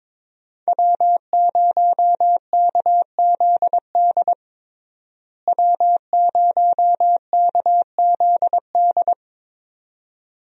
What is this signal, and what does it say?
Morse code, heard as a keyed tone, about 21 wpm, 700 Hz. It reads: W0KZD W0KZD
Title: USA Call Signs - 5 Characters - Repeated 2x - 22wpm